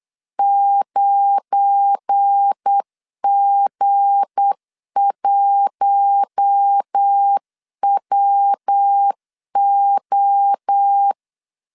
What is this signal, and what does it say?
Morse code, heard as a keyed tone, about 8 wpm, 800 Hz. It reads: 9G1WO
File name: LEZIONI / 7 GF9Z / 7 GF9 8wpm 3545